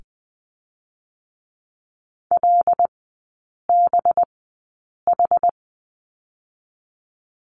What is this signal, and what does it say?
Morse code, heard as a keyed tone, about 20 wpm, 700 Hz. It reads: L B H
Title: 184 Lesson 30e - Letter b - Three Characters 20wpm